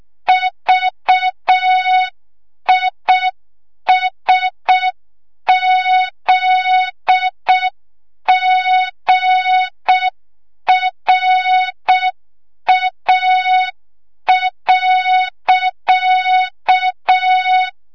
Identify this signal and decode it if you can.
Morse code, heard as a keyed tone, about 6 wpm, 3700 Hz. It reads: VISZGRA.
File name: Morze-HA2MN / hang / 25 GYAKORLAT